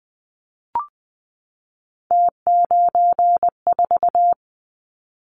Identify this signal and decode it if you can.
Morse code, heard as a keyed tone, about 20 wpm, 700 Hz. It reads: T94
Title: Callsign Prefixes for Top 100 DXCC - 20wpm